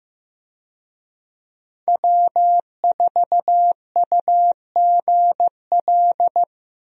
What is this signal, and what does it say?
Morse code, heard as a keyed tone, about 15 wpm, 700 Hz. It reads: W4UGL
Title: USA Call Signs - 5 Characters - Repeated 3x - 15wpm